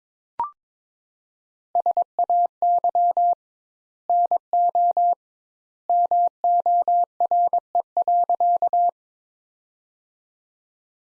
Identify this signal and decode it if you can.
Morse code, heard as a keyed tone, about 21 wpm, 700 Hz. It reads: SAY NO MORE.
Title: Sentences Top 200 Words - 22wpm